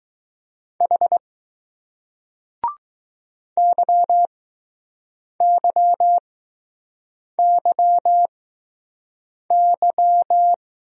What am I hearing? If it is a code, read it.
Morse code, heard as a keyed tone, about 20 wpm, spaced as for 6 wpm, 700 Hz. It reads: H YYYY